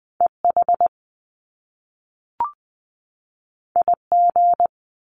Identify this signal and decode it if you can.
Morse code, heard as a keyed tone, about 20 wpm, 700 Hz. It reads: EH IG
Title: Top Two Letter Combinations in Words 20wpm